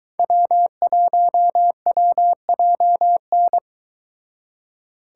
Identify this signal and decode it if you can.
Morse code, heard as a keyed tone, about 23 wpm, 700 Hz. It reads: W1WJN